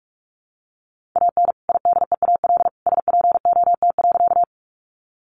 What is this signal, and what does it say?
Morse code, heard as a keyed tone, about 44 wpm, 700 Hz. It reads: AN IDEAL SPOT.
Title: Sentences from Top 1400 Words 45wpm